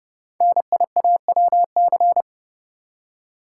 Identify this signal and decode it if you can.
Morse code, heard as a keyed tone, about 30 wpm, 690 Hz. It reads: NIAWC
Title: Code Groups - 5 Letters - 30wpm